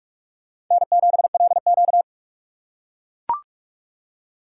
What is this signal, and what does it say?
Morse code, heard as a keyed tone, about 45 wpm, 700 Hz. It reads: N7LQ
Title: USA Call Signs - 4 Characters - 2x Character Spacing - 45wpm